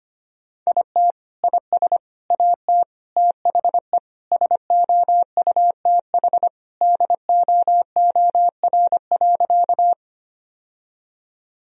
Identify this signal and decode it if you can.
Morse code, heard as a keyed tone, about 25 wpm, 700 Hz. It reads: IT IS AT THE SOUTH DOOR.
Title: Sentences from Top 1000 Words - Review Part 2 - 25wpm